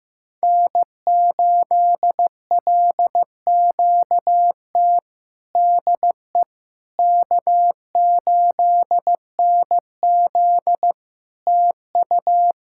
Morse code, heard as a keyed tone, about 15 wpm, 700 Hz. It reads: N8LQT DE K8NZ TU
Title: Practice QSO - Exchange 4 - Ending - 15wpm